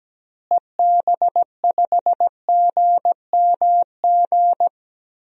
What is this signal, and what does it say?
Morse code, heard as a keyed tone, about 17 wpm, 700 Hz. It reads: EB5GMG
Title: DXCC-100 Call Signs - 6 Characters - Repeated 3x - 17wpm